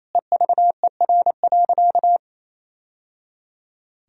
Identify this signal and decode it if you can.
Morse code, heard as a keyed tone, about 28 wpm, 690 Hz. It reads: EVER.
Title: Sentences from Top 1000 Words - Review Part 2 - 28wpm